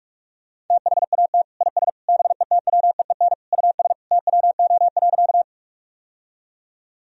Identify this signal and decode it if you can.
Morse code, heard as a keyed tone, about 46 wpm, 700 Hz. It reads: THAT IS BETWEEN US TWO.
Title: Sentences from Top 1000 Words - Review Part 1 - 45wpm